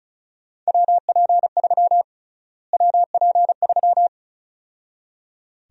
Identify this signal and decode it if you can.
Morse code, heard as a keyed tone, about 36 wpm, 700 Hz. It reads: WP3 WP3
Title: USA Call Sign Prefixes - Repeated 2x - 35wpm